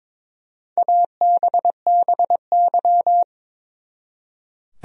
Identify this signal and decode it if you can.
Morse code, heard as a keyed tone, about 21 wpm, 700 Hz. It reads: ABBY
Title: USA Common Names - Max 4 Letters - 22wpm